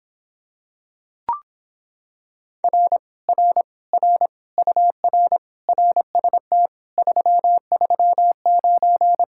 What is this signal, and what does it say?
Morse code, heard as a keyed tone, about 26 wpm, 700 Hz. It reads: R R R UR RST 339